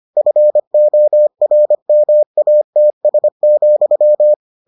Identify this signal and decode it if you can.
Morse code, heard as a keyed tone, about 25 wpm, 600 Hz. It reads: FORMATS,